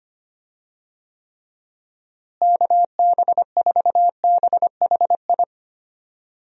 Morse code, heard as a keyed tone, about 25 wpm, 700 Hz. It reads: KB4BHI
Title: USA Call Signs - 6 Characters - Repeated 2x - 25wpm